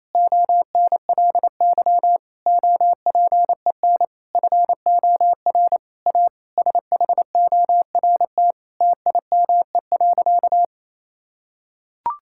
Morse code, heard as a keyed tone, about 28 wpm, 700 Hz. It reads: ONLY OPEN FOR A SHORT TIME.
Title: Sentences from Top 1600 Words 28wpm